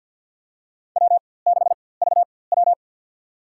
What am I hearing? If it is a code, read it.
Morse code, heard as a keyed tone, about 53 wpm, spaced as for 24 wpm, 700 Hz. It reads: W6VW
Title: USA Call Signs - 4x Character Spacing - 50wpm